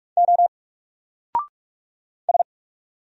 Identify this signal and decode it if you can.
Morse code, heard as a keyed tone, about 45 wpm, 700 Hz. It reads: O S
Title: Single Letter-Number 45wpm